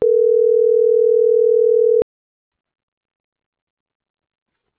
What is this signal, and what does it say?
Ringback tone (North America), 440 + 480 Hz.